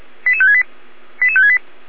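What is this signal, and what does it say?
Five-tone selective call (ZVEI), heard as quick runs of five tones, tones of about 80 ms. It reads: 88568 88568
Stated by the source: ZVEI five-tone call received by SDR, feuerwehr-orlamuende.org (fire brigade Orlamünde).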